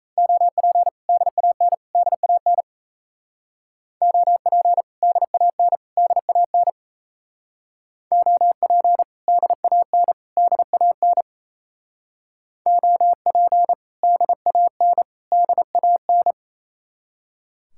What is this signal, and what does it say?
Morse code, heard as a keyed tone, about 34 wpm, 700 Hz. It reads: OP DAN DAN OP DAN DAN OP DAN DAN OP DAN DAN
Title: Speed-Racing Common QSO Phrases - 28wpm